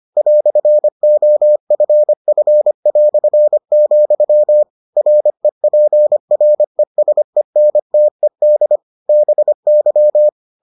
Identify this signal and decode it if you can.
Morse code, heard as a keyed tone, about 25 wpm, 600 Hz. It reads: "OFF", REPRESENTED BY